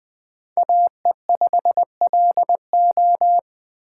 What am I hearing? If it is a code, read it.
Morse code, heard as a keyed tone, about 24 wpm, 700 Hz. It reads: AE5LO
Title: Speed Racing - USA 2x2 Call Signs 20wpm